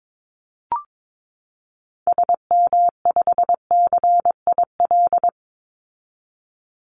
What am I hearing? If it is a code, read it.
Morse code, heard as a keyed tone, about 22 wpm, 700 Hz. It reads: SM5CIL
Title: Call Signs - 22wpm